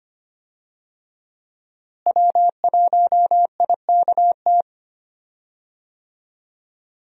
Morse code, heard as a keyed tone, about 25 wpm, 700 Hz. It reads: W1IKT
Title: USA Call Signs - 5 Characters - Repeated 2x - 25wpm